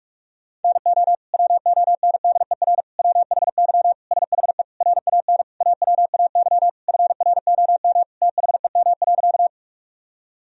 Morse code, heard as a keyed tone, about 43 wpm, 700 Hz. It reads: NO WONDER WHY SHE RAN AWAY FROM THEM.
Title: Sentences from Top 1000 Words - Review Part 3 - 45wpm